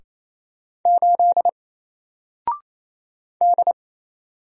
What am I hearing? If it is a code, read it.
Morse code, heard as a keyed tone, about 28 wpm, 700 Hz. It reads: 8 D
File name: Single-Letter-Number-28wpm